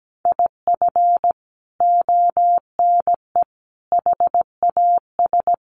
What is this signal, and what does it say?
Morse code, heard as a keyed tone, about 17 wpm, 700 Hz. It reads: IF ONE HAS
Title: Sentences from Top 1400 Words 17wpm